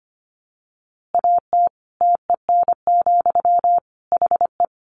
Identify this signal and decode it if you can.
Morse code, heard as a keyed tone, about 25 wpm, 700 Hz. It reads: AT TEN, HE
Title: Easy Phrases 25wpm